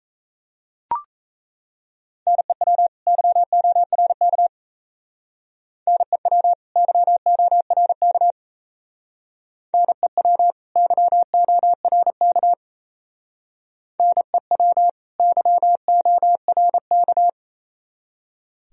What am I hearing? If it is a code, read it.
Morse code, heard as a keyed tone, about 33 wpm, 700 Hz. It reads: NEW YORK NEW YORK NEW YORK NEW YORK
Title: Speed-Racing - Sets of 2 Words 28wpm